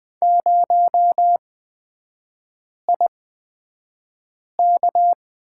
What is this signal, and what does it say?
Morse code, heard as a keyed tone, about 20 wpm, spaced as for 5 wpm, 700 Hz. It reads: 0IK
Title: 243 Lesson 39e - Letter z - Three Characters 20wpm